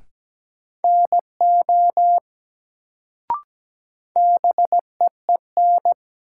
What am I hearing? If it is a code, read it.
Morse code, heard as a keyed tone, about 17 wpm, 700 Hz. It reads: NO BEEN
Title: Sentences from Top 100 Words - New Word Drill - 17wpm